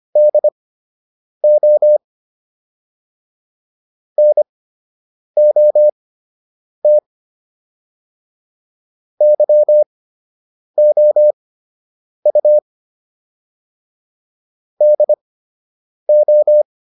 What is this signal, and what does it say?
Morse code, heard as a keyed tone, about 25 wpm, spaced as for 8 wpm, 600 Hz. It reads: DO NOT YOU DO